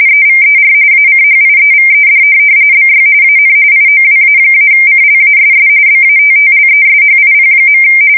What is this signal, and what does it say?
RTTY, about 45.45 baud. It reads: EVER ACHIEVED BY AN IOWA-CLASS BATTLESHIP.
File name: fsk170 / USS-New-Jersey